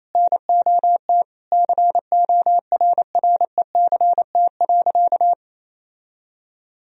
Morse code, heard as a keyed tone, about 28 wpm, 700 Hz. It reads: NOT CORRECT.
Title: Sentences from Top 1000 Words - Review Part 3 - 28wpm